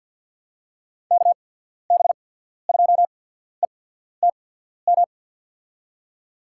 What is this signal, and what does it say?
Morse code, heard as a keyed tone, about 54 wpm, spaced as for 14 wpm, 700 Hz. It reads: KB2ETM
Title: USA Call Signs - 6 Characters - 8x Character Spacing - 50wpm